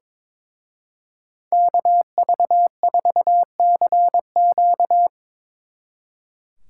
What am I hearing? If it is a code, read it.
Morse code, heard as a keyed tone, about 24 wpm, 700 Hz. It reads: KV4CQ
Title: Speed Racing - USA Call Signs 22wpm